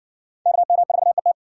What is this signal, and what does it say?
Morse code, heard as a keyed tone, about 60 wpm, 700 Hz. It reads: KM4ET